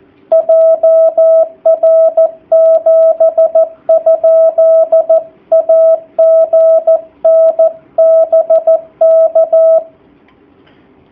Morse code, heard as a keyed tone, about 14 wpm, 660 Hz. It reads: JR7? AGN BK
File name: jh8cbh / 2019cw / sp13- w